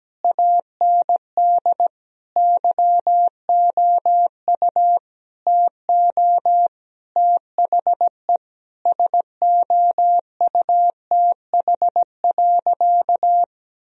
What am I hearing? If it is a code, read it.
Morse code, heard as a keyed tone, about 17 wpm, 700 Hz. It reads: AND YOU TO THE SOUTH.
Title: Sentences from Top 1000 Words - Review Part 2 - 17wpm